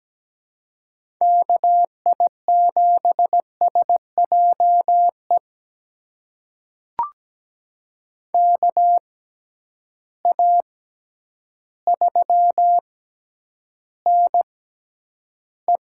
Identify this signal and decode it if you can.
Morse code, heard as a keyed tone, about 17 wpm, spaced as for 9 wpm, 700 Hz. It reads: KI7SJE KA3NE
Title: USA Call Signs - 6 Characters - 6x Character Spacing - 17wpm